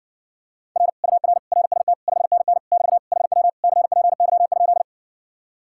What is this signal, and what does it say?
Morse code, heard as a keyed tone, about 61 wpm, 700 Hz. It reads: R UR RST 5NN <BT> HW CPY?